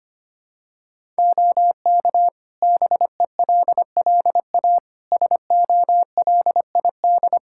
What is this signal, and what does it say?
Morse code, heard as a keyed tone, about 25 wpm, 700 Hz. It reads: OK BELLA SOLID
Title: K1USN SST QSO Practice 25wpm